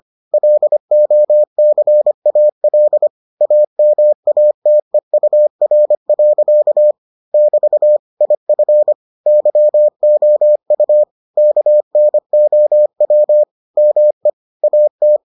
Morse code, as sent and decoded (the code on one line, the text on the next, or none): .-.. --- -.-. .- .-.. / .- -- .- - . ..- .-. .-.-.- / -...- / .. ..-. / -.-- --- ..- / -.- -. --- .-- / -- . / .- -
LOCAL AMATEUR. <BT> IF YOU KNOW ME AT